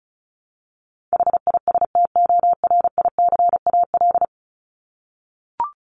Morse code, heard as a keyed tone, about 34 wpm, 700 Hz. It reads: HISTORICAL
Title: Top 501-750 Words 35wpm